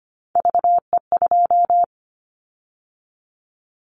Morse code, heard as a keyed tone, about 25 wpm, 700 Hz. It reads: VE2
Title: Callsign Prefixes for Top 25 DXCC - 25wpm